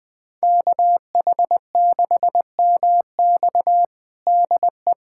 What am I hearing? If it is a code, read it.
Morse code, heard as a keyed tone, about 21 wpm, 700 Hz. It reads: KH6MX DE
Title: USA Callsign DE Callsign - 5 Characters - 20wpm